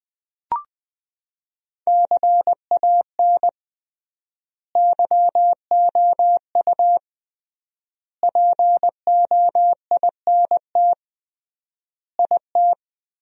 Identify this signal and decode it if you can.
Morse code, heard as a keyed tone, about 20 wpm, 700 Hz. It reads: CAN YOU POINT IT
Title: 3x Word-Spacing - Sentences from Top 200 Words - 20wpm